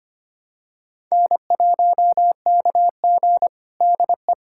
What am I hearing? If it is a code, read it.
Morse code, heard as a keyed tone, about 25 wpm, 700 Hz. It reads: N1KG DE
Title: USA Callsign DE Callsign - 4 Characters - 25wpm